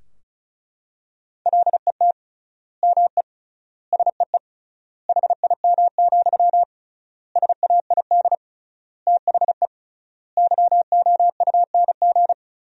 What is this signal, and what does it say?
Morse code, heard as a keyed tone, about 36 wpm, 700 Hz. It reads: LET ME SEE HIM, SAID THE YOUNG
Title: 3x Word-Spacing - Sentences from Top 400 Words - 35wpm